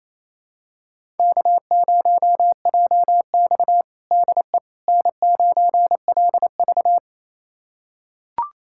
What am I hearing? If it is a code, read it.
Morse code, heard as a keyed tone, about 28 wpm, 700 Hz. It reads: K0JX DE N9LV
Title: USA Callsign DE Callsign - 4 Characters - 28wpm